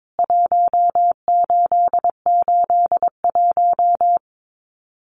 Morse code, heard as a keyed tone, about 22 wpm, 700 Hz. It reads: 1881
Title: CWT Exchange 22wpm